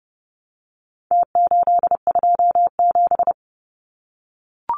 Morse code, heard as a keyed tone, about 30 wpm, 700 Hz. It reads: T827
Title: Numbers - Contest Style - 4-Digit 30wpm